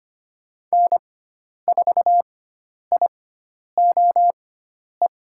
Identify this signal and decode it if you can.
Morse code, heard as a keyed tone, about 25 wpm, spaced as for 10 wpm, 700 Hz. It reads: N4IOE
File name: Lesson 16g - Letter c - Call Signs 25wpm